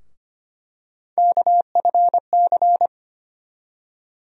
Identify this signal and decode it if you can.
Morse code, heard as a keyed tone, about 25 wpm, 700 Hz. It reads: KFC
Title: USA Call Sign Postfixes - 3 Letters 25wpm